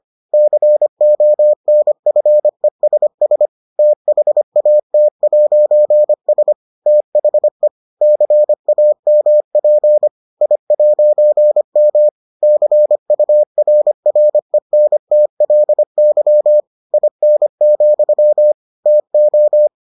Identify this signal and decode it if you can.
Morse code, heard as a keyed tone, about 25 wpm, 600 Hz. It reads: CONFESS THAT'S THE CAMP I'M CURRENTLY IN, TO